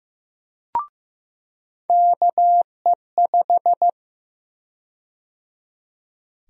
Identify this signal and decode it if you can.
Morse code, heard as a keyed tone, about 16 wpm, 700 Hz. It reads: KE5
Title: USA Call Sign Prefixes - 15wpm